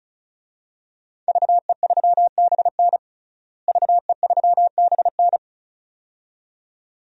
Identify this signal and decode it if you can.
Morse code, heard as a keyed tone, about 34 wpm, 700 Hz. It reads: VE3BN VE3BN
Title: DXCC-100 Call Signs - 5 Characters - Repeated 2x - 35wpm